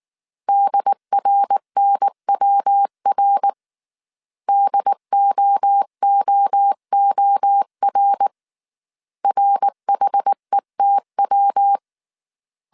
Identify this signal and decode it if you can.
Morse code, heard as a keyed tone, about 19 wpm, 800 Hz. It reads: DRNWR DOOOR RHETW